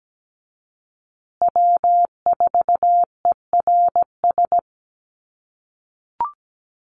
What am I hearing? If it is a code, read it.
Morse code, heard as a keyed tone, about 17 wpm, 700 Hz. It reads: W4ERS
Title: USA Call Signs Repeated 2x 17wpm